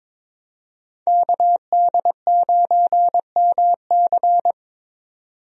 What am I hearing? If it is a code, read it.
Morse code, heard as a keyed tone, about 22 wpm, 700 Hz. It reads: KD9MC
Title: USA Call Signs - 5 Characters - 22wpm